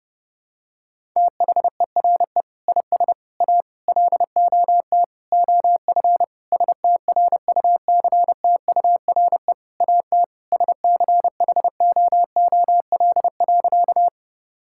Morse code, heard as a keyed tone, about 32 wpm, 700 Hz. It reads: THERE IS A LOT OF STRUCTURE AT SCHOOL.
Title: Sentences from Top 1000 Words - Review Part 6 - 30wpm